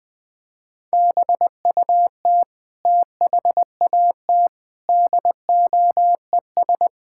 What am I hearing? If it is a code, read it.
Morse code, heard as a keyed tone, about 20 wpm, 700 Hz. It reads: BUT THAT DOES
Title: Sentences with 5-Word Phrases - Spoken Beginning - 20wpm